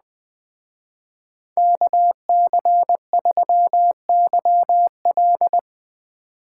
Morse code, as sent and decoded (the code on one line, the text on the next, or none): -.- -.-. ...-- -.-- .-..
KC3YL